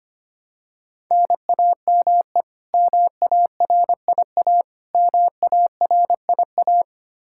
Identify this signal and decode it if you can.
Morse code, heard as a keyed tone, about 25 wpm, 700 Hz. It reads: NAME MARIA MARIA
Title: Practice QSO - Exchange 1 - Beginning - 25wpm